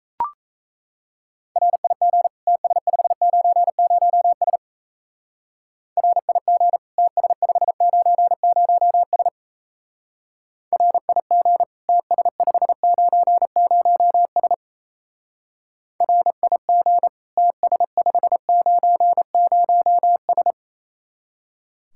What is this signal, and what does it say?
Morse code, heard as a keyed tone, about 34 wpm, 700 Hz. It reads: RIG TS590S RIG TS590S RIG TS590S RIG TS590S
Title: Speed-Racing Common QSO Phrases Encore - 28wpm